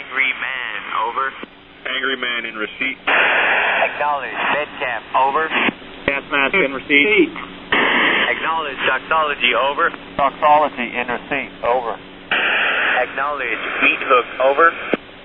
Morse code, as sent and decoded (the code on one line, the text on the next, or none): none